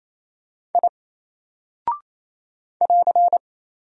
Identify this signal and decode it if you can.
Morse code, heard as a keyed tone, about 27 wpm, 700 Hz. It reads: I <AR>
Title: Single Character (Letter:Number:Punctuation:Prosign) 28wpm